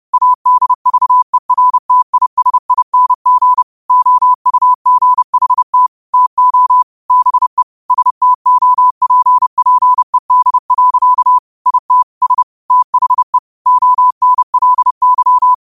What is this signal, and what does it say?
Morse code, heard as a keyed tone, about 30 wpm, 1000 Hz. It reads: ADVERTISING OUGHT TO BE STOPPED. IT S THE ONLY
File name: qotdfiles / 30wpm